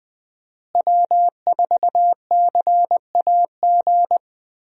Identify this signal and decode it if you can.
Morse code, heard as a keyed tone, about 20 wpm, 700 Hz. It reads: W4CAG
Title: USA Call Signs - 5 Characters - Repeated 2x - 20wpm